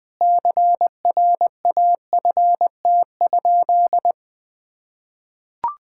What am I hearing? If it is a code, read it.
Morse code, heard as a keyed tone, about 20 wpm, 700 Hz. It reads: CRAFT?
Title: Sentences from Top 1000 Words - Review Part 6 - 20wpm